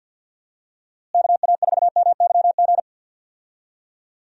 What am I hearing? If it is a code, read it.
Morse code, heard as a keyed tone, about 48 wpm, 700 Hz. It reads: KA4MYG
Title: USA Call Signs - 6 Characters - Repeated 3x - 50wpm